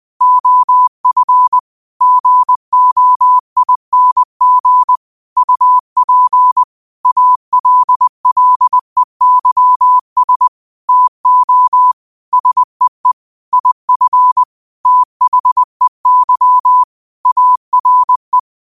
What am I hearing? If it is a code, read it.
Morse code, heard as a keyed tone, about 20 wpm, 1000 Hz. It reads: OF GOING UP ALLEYS TO SEE IF THEY ARE